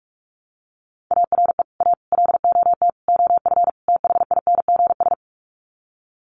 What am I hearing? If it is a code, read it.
Morse code, heard as a keyed tone, about 44 wpm, 700 Hz. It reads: ARE A LOT OF THINGS
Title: Sets of 5 Words Encore2 45wpm